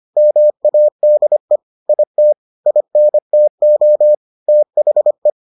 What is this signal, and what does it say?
Morse code, heard as a keyed tone, about 25 wpm, 600 Hz. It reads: MADE IT INTO THE